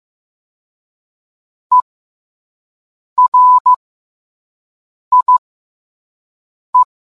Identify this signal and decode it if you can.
Morse code, heard as a keyed tone, about 15 wpm, spaced as for 5 wpm, 1000 Hz. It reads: ERIE